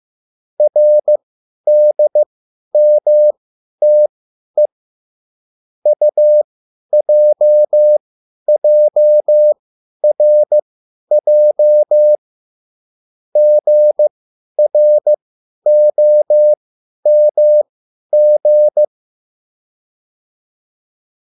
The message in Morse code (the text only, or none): RDMTE UJJRJ GROMG